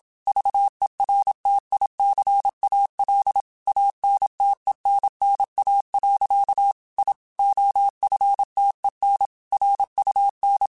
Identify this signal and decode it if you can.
Morse code, heard as a keyed tone, about 27 wpm, 800 Hz. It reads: VERTICAL ANTENNA. I OFTEN RUN
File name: QSOs / QSOs-mp3s / kc7sac@25